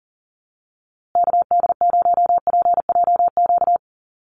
Morse code, heard as a keyed tone, about 40 wpm, 700 Hz. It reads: KD0PJQ
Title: USA Call Signs Repeated 2x 40wpm